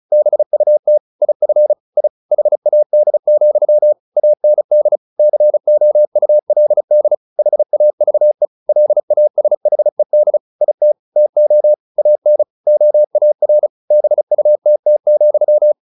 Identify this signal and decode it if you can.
Morse code, heard as a keyed tone, about 34 wpm, 600 Hz. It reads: BUT IF I HAD, AND COULD HAVE LASHED IT TO AN OAR BUTT,
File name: the old man and the sea 2 35wpm